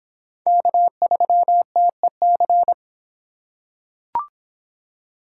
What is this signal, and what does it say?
Morse code, heard as a keyed tone, about 26 wpm, 700 Hz. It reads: K3TEC